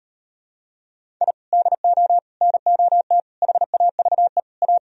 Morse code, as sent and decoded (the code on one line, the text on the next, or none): .. / -.. --- / -. --- - / .... .- ...- . / .-
I DO NOT HAVE A